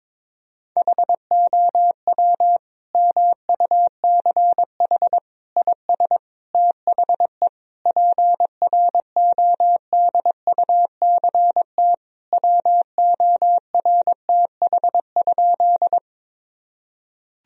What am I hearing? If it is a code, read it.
Morse code, heard as a keyed tone, about 22 wpm, 700 Hz. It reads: HOW MUCH IS THE PRODUCT WORTH?
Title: Sentences from Top 1000 Words - Review Part 2 - 22wpm